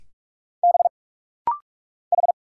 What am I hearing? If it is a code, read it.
Morse code, heard as a keyed tone, about 46 wpm, 700 Hz. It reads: B H
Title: Single Letter-Number Alt - Rapid-Fire 45wpm